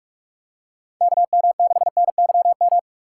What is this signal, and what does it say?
Morse code, heard as a keyed tone, about 45 wpm, 700 Hz. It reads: KM6NYM